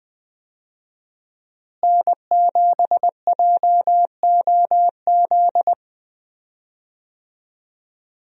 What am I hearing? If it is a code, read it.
Morse code, heard as a keyed tone, about 20 wpm, 700 Hz. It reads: N7JOZ
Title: USA Call Signs - 5 Characters - Repeated 2x - 20wpm